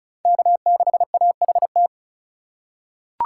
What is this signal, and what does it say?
Morse code, heard as a keyed tone, about 36 wpm, 700 Hz. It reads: K6AHT
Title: USA Call Signs - 5 Characters - 2x Character Spacing - 35wpm